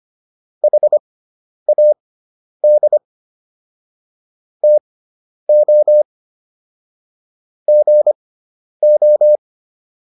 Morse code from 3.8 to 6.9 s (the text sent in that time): TO